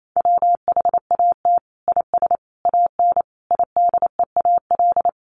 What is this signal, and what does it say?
Morse code, heard as a keyed tone, about 28 wpm, 700 Hz. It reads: WHAT IS AN IDEAL
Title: Sentences from Top 1400 Words 28wpm